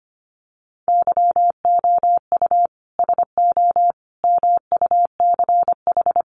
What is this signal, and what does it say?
Morse code, heard as a keyed tone, about 25 wpm, 700 Hz. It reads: YOU SO MUCH
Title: Sets of 5 Words Encore2 25wpm